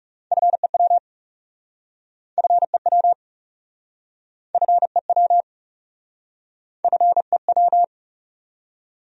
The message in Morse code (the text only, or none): FEW FEW FEW FEW